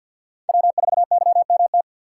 Morse code, heard as a keyed tone, about 56 wpm, 700 Hz. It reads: W4YMT
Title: Speed Racing - USA Call Signs 50wpm